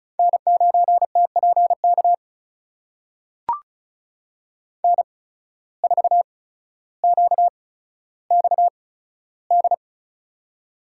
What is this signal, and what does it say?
Morse code, heard as a keyed tone, about 36 wpm, spaced as for 15 wpm, 700 Hz. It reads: N9TPK N4QXD
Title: USA Call Signs - 5 Characters - 8x Character Spacing - 35wpm